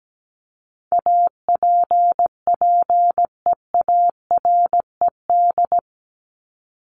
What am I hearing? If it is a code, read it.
Morse code, heard as a keyed tone, about 17 wpm, 700 Hz. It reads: APPEARED
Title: ED Suffix - Top 100 Words with Base Words 17wpm